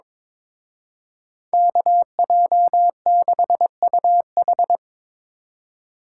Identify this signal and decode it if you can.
Morse code, heard as a keyed tone, about 26 wpm, 700 Hz. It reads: KJ6UH
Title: Speed Racing - USA 2x2 Call Signs 22wpm